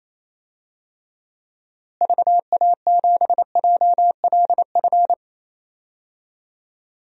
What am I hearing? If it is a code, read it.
Morse code, heard as a keyed tone, about 28 wpm, 700 Hz. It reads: VA7JLF
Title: DXCC-100 Call Signs - 6 Characters - Repeated 2x - 28wpm